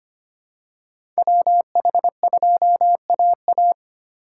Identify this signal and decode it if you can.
Morse code, heard as a keyed tone, about 25 wpm, 700 Hz. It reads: WH2AA
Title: USA Call Signs - 5 Characters - 4x Character Spacing - 25wpm